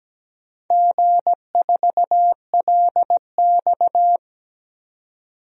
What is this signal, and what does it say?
Morse code, heard as a keyed tone, about 17 wpm, 700 Hz. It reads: G4LX
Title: DXCC-100 Call Signs - 4 Characters - 17wpm